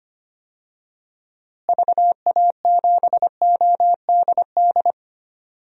Morse code, heard as a keyed tone, about 25 wpm, 700 Hz. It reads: VA7ODD
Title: DXCC-100 Call Signs - 6 Characters - Repeated 3x - 25wpm